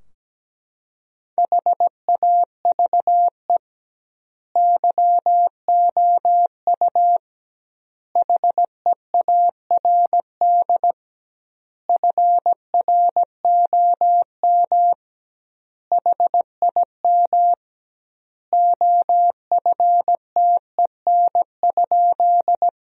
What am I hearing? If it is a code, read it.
Morse code, heard as a keyed tone, about 17 wpm, 700 Hz. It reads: HAVE YOU HEARD FROM HIM OFTEN?
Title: 2x Word-Spacing - Sentences from Top 400 Words - 17wpm